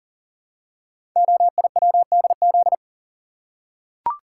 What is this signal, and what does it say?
Morse code, heard as a keyed tone, about 43 wpm, 700 Hz. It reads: OIWDZ
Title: Code Groups - 5 Letters - 40wpm